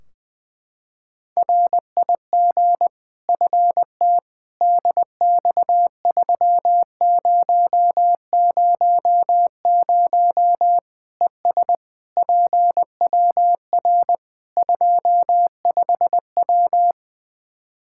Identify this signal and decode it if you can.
Morse code, heard as a keyed tone, about 20 wpm, 700 Hz. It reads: RIG FT DX3000 ES PWR 25W
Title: K1USN SST QSO Practice 20wpm